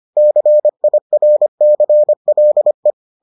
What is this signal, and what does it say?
Morse code, heard as a keyed tone, about 25 wpm, 600 Hz. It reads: CIRCLE